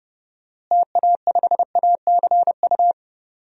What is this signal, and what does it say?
Morse code, heard as a keyed tone, about 30 wpm, 700 Hz. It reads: TA5ACU
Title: DXCC-100 Call Signs - 6 Characters - Repeated 2x - 30wpm